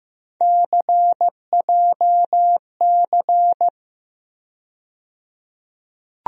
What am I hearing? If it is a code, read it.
Morse code, heard as a keyed tone, about 15 wpm, 700 Hz. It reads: CJC
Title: USA Call Sign Postfixes - 3 Letters 15wpm